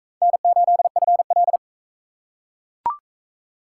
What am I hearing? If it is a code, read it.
Morse code, heard as a keyed tone, about 42 wpm, 700 Hz. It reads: N8FL